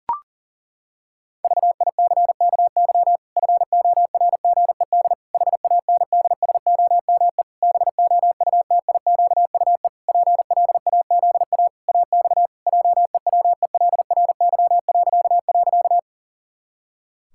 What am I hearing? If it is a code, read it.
Morse code, heard as a keyed tone, about 40 wpm, 700 Hz. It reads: VICKY FORGED HANDSOME BOUTIQUE PLAZA AX JEWELRY..
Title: Pangram Challenge v2 - 40wpm